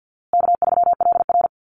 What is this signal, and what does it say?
Morse code, heard as a keyed tone, about 50 wpm, 700 Hz. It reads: K3LR
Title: Easy Call Signs Repeated 3x - 50wpm